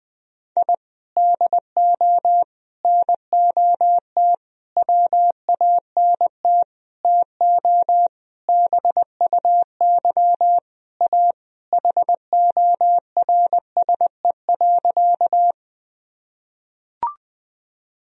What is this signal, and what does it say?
Morse code, heard as a keyed tone, about 20 wpm, 700 Hz. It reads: I DO NOT WANT TO BUY A HORSE.
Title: Sentences from Top 1000 Words - Review Part 4 - 20wpm